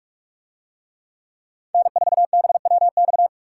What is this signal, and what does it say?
Morse code, heard as a keyed tone, about 46 wpm, 700 Hz. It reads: N4BWX